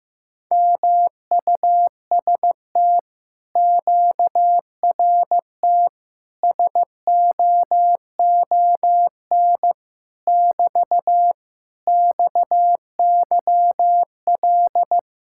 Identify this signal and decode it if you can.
Morse code, heard as a keyed tone, about 15 wpm, 700 Hz. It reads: MUST QRT SOON <BT> XYL
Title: Speed-Racing Common QSO Phrases Encore - 15wpm